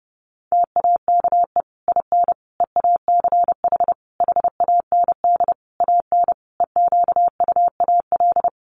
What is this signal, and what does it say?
Morse code, heard as a keyed tone, about 30 wpm, 700 Hz. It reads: TAKE IN EACH HAND AN EQUAL
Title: Sentences from Top 1400 Words 30wpm